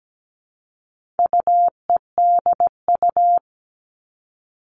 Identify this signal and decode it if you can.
Morse code, heard as a keyed tone, about 17 wpm, 700 Hz. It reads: UEDU
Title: All Cut Numbers - 4-Digit 17wpm